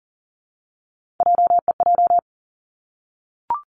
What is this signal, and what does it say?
Morse code, heard as a keyed tone, about 40 wpm, 700 Hz. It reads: JEJ
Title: Three Challenging Letter-Number Combinations 40wpm